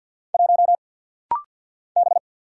Morse code, heard as a keyed tone, about 48 wpm, 700 Hz. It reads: 1 B
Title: Single Letter-Number - Rapid-Fire 50wpm